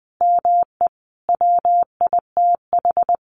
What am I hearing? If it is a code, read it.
Morse code, heard as a keyed tone, about 20 wpm, 700 Hz. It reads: ME WITH
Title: Sentences from Top 1500 Words 20wpm